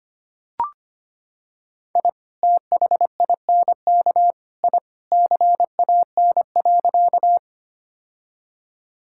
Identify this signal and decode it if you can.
Morse code, heard as a keyed tone, about 25 wpm, 700 Hz. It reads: I THINK I CAN.
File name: practice / 25wpm / Sentences with 3-Word Phrases - 25wpm